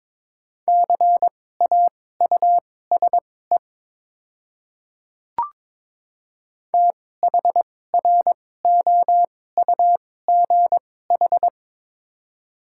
Morse code, heard as a keyed tone, about 22 wpm, spaced as for 16 wpm, 700 Hz. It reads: CAUSE THROUGH
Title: 2x Character Spacing - Top 300 Words - 22wpm